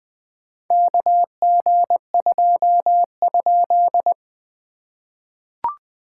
Morse code, heard as a keyed tone, about 20 wpm, 700 Hz. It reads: KG2?
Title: POTA QSO Practice - 20wpm